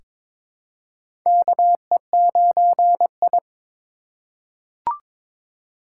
Morse code, heard as a keyed tone, about 22 wpm, 700 Hz. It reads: KE9I
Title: K1USN SST QSO Practice - 22wpm